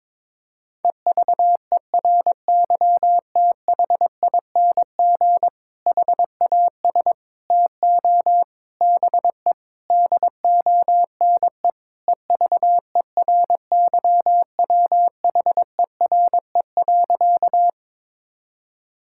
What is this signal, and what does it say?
Morse code, heard as a keyed tone, about 22 wpm, 700 Hz. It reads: EVERYTHING HAS TO BE DONE EVERYWHERE.
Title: Sentences from Top 1600 Words 22wpm